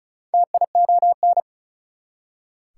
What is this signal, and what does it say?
Morse code, heard as a keyed tone, about 48 wpm, spaced as for 42 wpm, 700 Hz. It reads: TION
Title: Speed-Racing Prefixes and Suffixes 35wpm